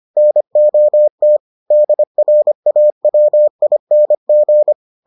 Morse code, heard as a keyed tone, about 25 wpm, 600 Hz. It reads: NOT DRAWING